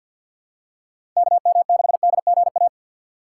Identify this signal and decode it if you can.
Morse code, heard as a keyed tone, about 51 wpm, 700 Hz. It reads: KM6DGA
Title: USA Call Signs - 6 Characters - 3x Character Spacing - 50wpm